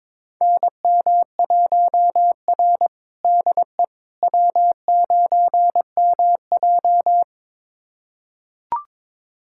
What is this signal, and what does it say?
Morse code, heard as a keyed tone, about 21 wpm, 700 Hz. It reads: NM1R DE W9MJ
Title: USA Callsign DE Callsign - 4 Characters - 22wpm